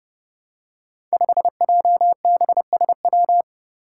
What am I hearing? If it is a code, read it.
Morse code, heard as a keyed tone, about 29 wpm, 700 Hz. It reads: 5JBSW
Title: Five Challenging Letter-Number Combinations 30wpm